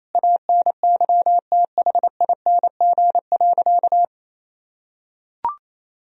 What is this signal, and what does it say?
Morse code, heard as a keyed tone, about 28 wpm, 700 Hz. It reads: ANYTHING.
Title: Sentences from Top 1000 Words - Review Part 2 - 28wpm